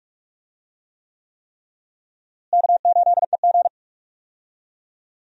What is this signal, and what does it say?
Morse code, heard as a keyed tone, about 43 wpm, 700 Hz. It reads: K8EG
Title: USA Call Signs - 4 Characters - Repeated 3x - 45wpm